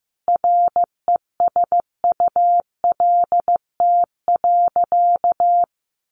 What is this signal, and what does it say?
Morse code, heard as a keyed tone, about 14 wpm, 700 Hz. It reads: RESULT.
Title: Sentences from Top 700 Words 15wpm